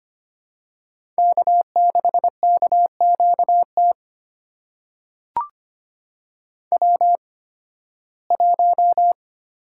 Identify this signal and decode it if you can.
Morse code, heard as a keyed tone, about 25 wpm, spaced as for 16 wpm, 700 Hz. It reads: K6KQT W1
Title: USA Call Signs - 5 Characters - 8x Character Spacing - 25wpm